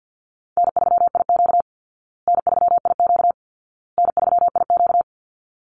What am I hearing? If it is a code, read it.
Morse code, heard as a keyed tone, about 50 wpm, 700 Hz. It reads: N3IQ N3IQ N3IQ
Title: Easy Call Signs Repeated 3x - 50wpm